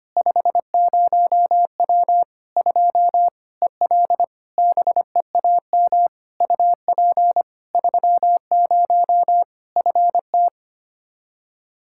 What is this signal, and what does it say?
Morse code, heard as a keyed tone, about 25 wpm, 700 Hz. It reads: 50W 2 EL BEAM UP 30 FT
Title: Practice QSO - Exchange 2 - Info - 25wpm